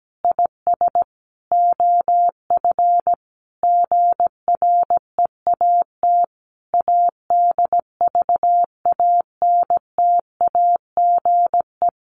Morse code, heard as a keyed tone, about 17 wpm, 700 Hz. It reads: IS OF GREAT ADVANTAGE
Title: Sentences from Top 900 Words 17wpm